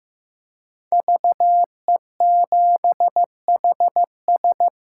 Morse code, heard as a keyed tone, about 15 wpm, 700 Hz. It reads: VE7HS
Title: K1USN SST QSO Practice - 15wpm